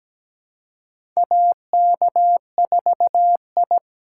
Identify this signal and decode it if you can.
Morse code, heard as a keyed tone, about 17 wpm, 700 Hz. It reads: AK4I
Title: USA Call Signs - 3x Character Spacing - 17wpm